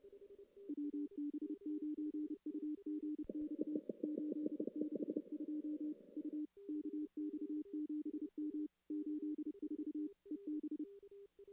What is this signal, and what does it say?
Morse code, heard as a keyed tone, about 30 wpm, 300 Hz. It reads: WB9UGX 862 U KX7M 84 EB
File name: contests / CW-SS-k5tr- -1600